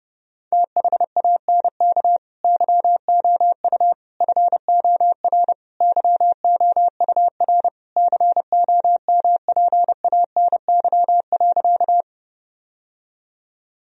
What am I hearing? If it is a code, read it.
Morse code, heard as a keyed tone, about 32 wpm, 700 Hz. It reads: THANK YOU FOR YOUR COMPANY.
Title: Sentences from Top 1000 Words - Review Part 3 - 30wpm